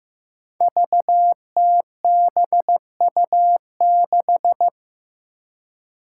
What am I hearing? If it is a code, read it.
Morse code, heard as a keyed tone, about 15 wpm, 700 Hz. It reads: VTBU6
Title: Five Challenging Letter-Number Combinations 15wpm